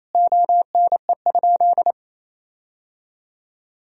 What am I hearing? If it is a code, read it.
Morse code, heard as a keyed tone, about 28 wpm, 700 Hz. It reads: ONE?
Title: Sentences from Top 1000 Words - Review Part 3 - 28wpm